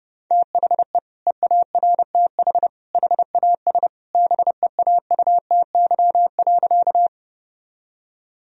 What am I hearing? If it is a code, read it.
Morse code, heard as a keyed tone, about 30 wpm, 700 Hz. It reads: THE EARTH HAS BEAUTY.
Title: Sentences from Top 1000 Words - Review Part 2 - 30wpm